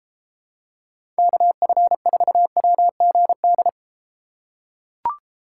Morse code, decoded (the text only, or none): KF4WGD